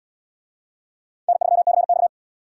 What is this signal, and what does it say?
Morse code, heard as a keyed tone, about 78 wpm, 700 Hz. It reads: N3CX